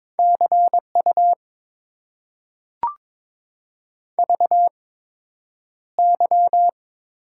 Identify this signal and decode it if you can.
Morse code, heard as a keyed tone, about 22 wpm, spaced as for 9 wpm, 700 Hz. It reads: CU VY
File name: Top 90 QSO Elements - 8x Character Spacing - 22wpm